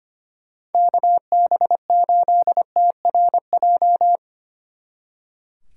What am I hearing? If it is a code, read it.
Morse code, heard as a keyed tone, about 30 wpm, 700 Hz. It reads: KB8TRJ
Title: Speed Racing - USA 2x3 Call Signs 25wpm